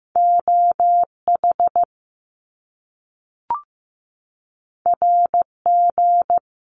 Morse code, heard as a keyed tone, about 15 wpm, 700 Hz. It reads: OH RG
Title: Top Two Letter Combinations in Words 15wpm